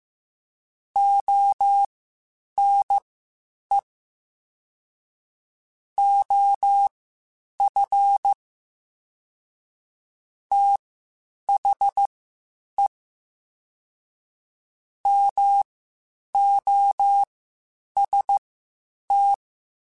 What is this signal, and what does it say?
Morse code, heard as a keyed tone, about 15 wpm, spaced as for 8 wpm, 800 Hz. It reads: ONE OF THE MOST
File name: Quotes Spaced / Quotes Spaced-mp3s / ChildAsleep sp @13